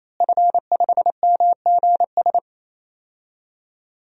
Morse code, heard as a keyed tone, about 28 wpm, 700 Hz. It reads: F5MGS
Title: DXCC-100 Call Signs - 5 Characters - Repeated 3x - 28wpm